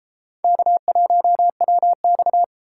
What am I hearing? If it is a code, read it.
Morse code, heard as a keyed tone, about 29 wpm, 700 Hz. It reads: K1WX